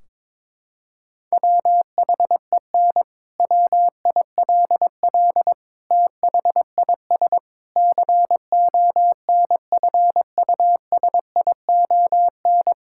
Morse code, heard as a keyed tone, about 22 wpm, 700 Hz. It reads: WHEN WILL THIS CONFUSION
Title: Sentences from Top 1600 Words 22wpm